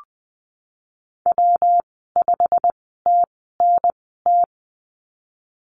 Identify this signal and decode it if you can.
Morse code, heard as a keyed tone, about 20 wpm, spaced as for 15 wpm, 700 Hz. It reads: W5TNT
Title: 147 Lesson 24g - Letter y - Call Signs 20wpm